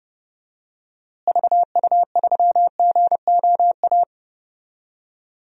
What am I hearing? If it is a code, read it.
Morse code, heard as a keyed tone, about 32 wpm, 700 Hz. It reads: VU3GOA
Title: DXCC-100 Call Signs - 6 Characters - Repeated 2x - 30wpm